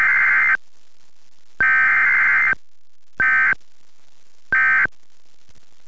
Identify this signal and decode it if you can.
Emergency Alert System (SAME) tones, heard as harsh buzzing bursts. It is heard from 1.0 to 5.2 s.
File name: listen / eas test